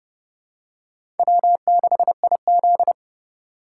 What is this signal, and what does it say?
Morse code, heard as a keyed tone, about 30 wpm, 700 Hz. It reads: W6IZ
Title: USA Call Signs - 4 Characters - 4x Character Spacing - 30wpm